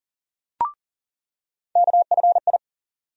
Morse code, heard as a keyed tone, about 40 wpm, 700 Hz. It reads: KFI
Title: USA Call Sign Postfixes - 3 Letters 40wpm